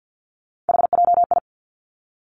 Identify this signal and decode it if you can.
Morse code, heard as a keyed tone, about 50 wpm, 700 Hz. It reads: HJI
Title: Three Challenging Letter-Number Combinations 50wpm